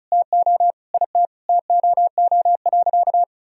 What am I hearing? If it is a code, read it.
Morse code, heard as a keyed tone, about 34 wpm, 700 Hz. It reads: TO IT TOO.
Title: Sentences from Top 1000 Words - Review Part 1 - 35wpm